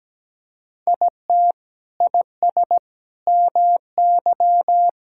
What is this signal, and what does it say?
Morse code, heard as a keyed tone, about 17 wpm, 700 Hz. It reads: IT IS MY
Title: Sentences from Top 1600 Words 17wpm